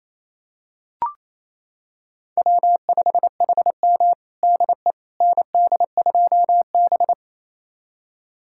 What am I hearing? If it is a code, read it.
Morse code, heard as a keyed tone, about 28 wpm, 700 Hz. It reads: W5HM DE ND2B
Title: USA Callsign DE Callsign - 4 Characters - 28wpm